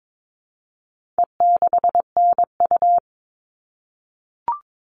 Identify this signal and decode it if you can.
Morse code, heard as a keyed tone, about 22 wpm, 700 Hz. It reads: E6NU
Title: All Cut Numbers - 4-Digit 22wpm